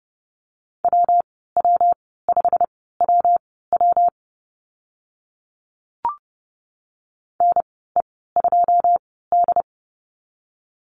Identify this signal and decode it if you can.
Morse code, heard as a keyed tone, about 30 wpm, spaced as for 17 wpm, 700 Hz. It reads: WW5WW NE2D
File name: Lesson 19g - Letter w - Call Signs 30wpm